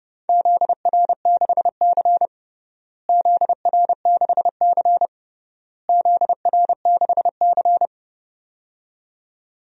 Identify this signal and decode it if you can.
Morse code, heard as a keyed tone, about 29 wpm, 700 Hz. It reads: ZR6C ZR6C ZR6C
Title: DXCC-100 Call Signs - 4 Characters - Repeated 3x - 30wpm